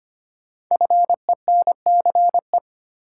Morse code, heard as a keyed tone, about 25 wpm, 700 Hz. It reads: FENCE
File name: practice / 25wpm / Easy Words with Hints - 25wpm